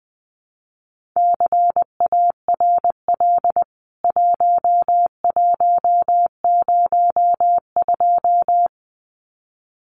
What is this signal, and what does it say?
Morse code, heard as a keyed tone, about 20 wpm, 700 Hz. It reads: CARL 1102
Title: CWT Exchange 20wpm